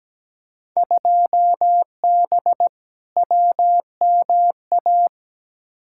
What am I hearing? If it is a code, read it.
Morse code, heard as a keyed tone, about 17 wpm, 700 Hz. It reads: 2B WMA
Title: Field Day Exchanges Encore - 17wpm